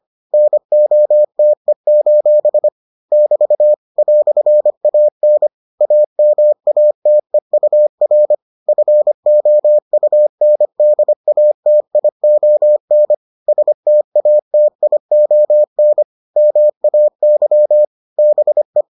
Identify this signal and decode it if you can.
Morse code, heard as a keyed tone, about 25 wpm, 600 Hz. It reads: NOTE: <BT> "AN AMATEUR FOUNDATION STATION MAY BE